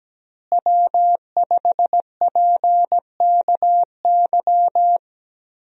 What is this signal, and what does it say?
Morse code, heard as a keyed tone, about 17 wpm, 700 Hz. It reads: W5PKY
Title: USA Call Signs - 5 Characters - Repeated 3x - 17wpm